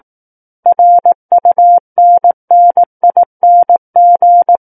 Morse code, heard as a keyed tone, about 18 wpm, 700 Hz. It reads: RUNNING
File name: Book One / 09-Chapter 9 - 18 WPM